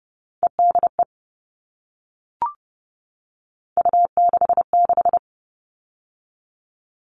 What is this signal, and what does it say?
Morse code, heard as a keyed tone, about 30 wpm, 700 Hz. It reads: EDE U66
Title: All Cut Numbers - 3-Digit 30wpm